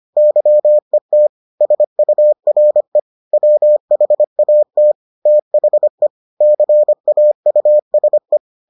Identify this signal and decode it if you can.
Morse code, heard as a keyed tone, about 25 wpm, 600 Hz. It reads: YET SURE WHAT THE CAUSE